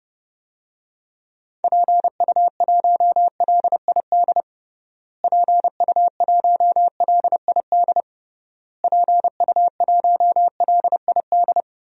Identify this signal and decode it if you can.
Morse code, heard as a keyed tone, about 29 wpm, 700 Hz. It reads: PU1LID PU1LID PU1LID
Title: DXCC-100 Call Signs - 6 Characters - Repeated 3x - 30wpm